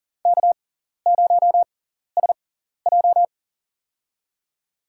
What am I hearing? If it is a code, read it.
Morse code, heard as a keyed tone, about 40 wpm, spaced as for 14 wpm, 700 Hz. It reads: K0SJ
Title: USA Call Signs - 4 Characters - 6x Character Spacing - 40wpm